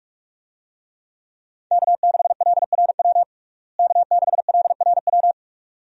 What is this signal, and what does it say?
Morse code, heard as a keyed tone, about 46 wpm, 700 Hz. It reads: K6LRW K6LRW
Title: USA Call Signs - 5 Characters - Repeated 2x - 45wpm